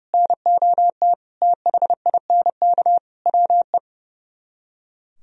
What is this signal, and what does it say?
Morse code, heard as a keyed tone, about 38 wpm, 700 Hz. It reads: NOT THINK WE